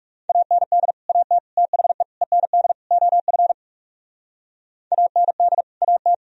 Morse code, heard as a keyed tone, about 36 wpm, 700 Hz. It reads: AND AT THE END OF AND AT
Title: Speed-Racing - Sets of 5 Words 30wpm